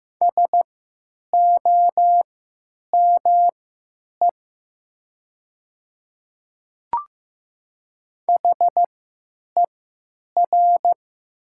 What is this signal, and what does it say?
Morse code, heard as a keyed tone, about 15 wpm, spaced as for 9 wpm, 700 Hz. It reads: SOME HER
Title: 3x Character Spacing - Top 100 Words - 15wpm